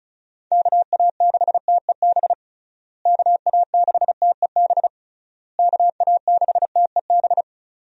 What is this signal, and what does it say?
Morse code, heard as a keyed tone, about 36 wpm, 700 Hz. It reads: KA6TEB KA6TEB KA6TEB
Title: USA Call Signs - 6 Characters - Repeated 3x - 35wpm